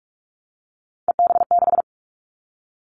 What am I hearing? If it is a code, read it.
Morse code, heard as a keyed tone, about 46 wpm, 700 Hz. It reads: EB6
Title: Callsign Prefixes for Top 50 DXCC - 45wpm